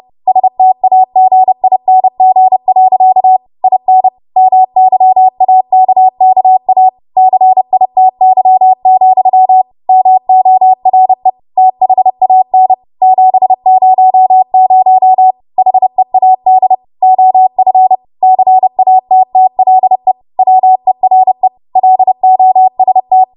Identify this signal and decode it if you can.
Morse code, heard as a keyed tone, about 30 wpm, 750 Hz. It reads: STAGING. IN MYAKKA CITY, MORE THAN 700 HEAD OF CATTLE WERE LOST